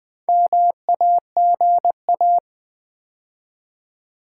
Morse code, heard as a keyed tone, about 20 wpm, 700 Hz. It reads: MAGA
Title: Build A Word - 2 Letters At A Time - 8-Letter Words - 20wpm